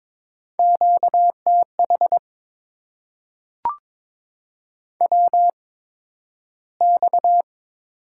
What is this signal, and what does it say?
Morse code, heard as a keyed tone, about 22 wpm, spaced as for 12 wpm, 700 Hz. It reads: QTH WX
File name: Top 90 QSO Elements - 8x Character Spacing - 22wpm